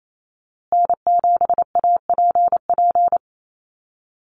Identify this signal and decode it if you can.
Morse code, heard as a keyed tone, about 28 wpm, 700 Hz. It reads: N7APP
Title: USA Call Signs Repeated 2x 28wpm